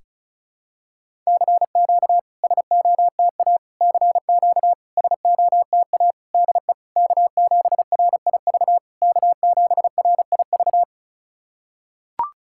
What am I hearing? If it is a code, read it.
Morse code, heard as a keyed tone, about 36 wpm, 700 Hz. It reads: CQ SOTA CQ SOTA DE K7RIV K7RIV
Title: SOTA QSO Practice 35wpm